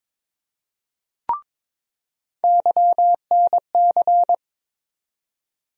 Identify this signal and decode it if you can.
Morse code, heard as a keyed tone, about 21 wpm, 700 Hz. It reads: YNC
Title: USA Call Sign Postfixes - 3 Letters 22wpm